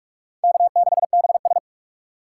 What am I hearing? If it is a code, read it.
Morse code, heard as a keyed tone, about 46 wpm, 700 Hz. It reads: K6BS
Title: USA Call Signs - 4 Characters - 6x Character Spacing - 45wpm